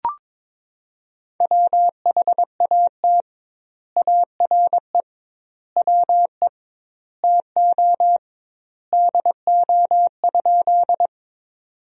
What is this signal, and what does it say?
Morse code, heard as a keyed tone, about 22 wpm, 700 Hz. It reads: WHAT ARE WE TO DO?
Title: 2x Word-Spacing - Sentences from Top 100 Words - 22wpm